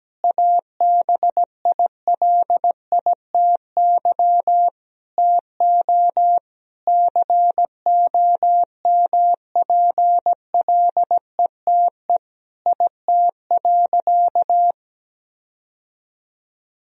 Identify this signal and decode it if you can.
Morse code, heard as a keyed tone, about 17 wpm, 700 Hz. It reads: ABILITY TO COMPLETE IT.
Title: Sentences from Top 1000 Words - Review Part 3 - 17wpm